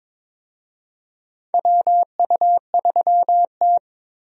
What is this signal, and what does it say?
Morse code, heard as a keyed tone, about 22 wpm, 700 Hz. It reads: WU3T
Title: USA Call Signs - 4 Characters - Repeated 2x - 22wpm